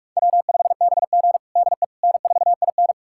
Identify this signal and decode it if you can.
Morse code, heard as a keyed tone, about 44 wpm, 700 Hz. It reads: W5BG DE N4IN
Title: USA Callsign DE Callsign - 4 Characters - 45wpm